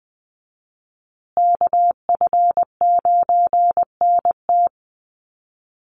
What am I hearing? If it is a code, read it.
Morse code, heard as a keyed tone, about 20 wpm, 700 Hz. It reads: KF9NT
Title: USA Call Signs Repeated 2x 20wpm